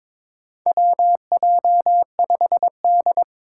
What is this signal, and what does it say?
Morse code, heard as a keyed tone, about 22 wpm, 700 Hz. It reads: WJ5D
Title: USA Call Signs - 4 Characters - 2x Character Spacing - 22wpm